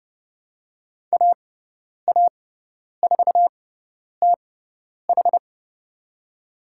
Morse code, heard as a keyed tone, about 30 wpm, spaced as for 10 wpm, 700 Hz. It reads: AA4TH
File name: Lesson 16g - Letter c - Call Signs 30wpm